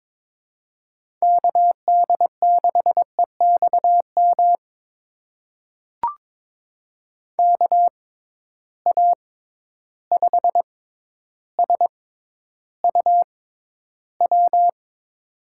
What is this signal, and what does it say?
Morse code, heard as a keyed tone, about 22 wpm, spaced as for 11 wpm, 700 Hz. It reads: KD6EXM KA5SUW
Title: USA Call Signs - 6 Characters - 6x Character Spacing - 22wpm